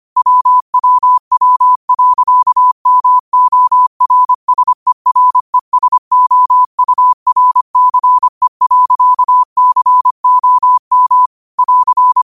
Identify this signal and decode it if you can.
Morse code, heard as a keyed tone, about 25 wpm, 1000 Hz. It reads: WWW.MORSERESOURCE.COM <AR>
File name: qotdfiles / 25wpm